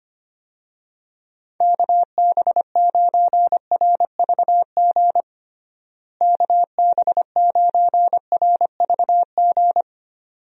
Morse code, heard as a keyed tone, about 25 wpm, 700 Hz. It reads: KB9RVG KB9RVG
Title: USA Call Signs - 6 Characters - Repeated 2x - 25wpm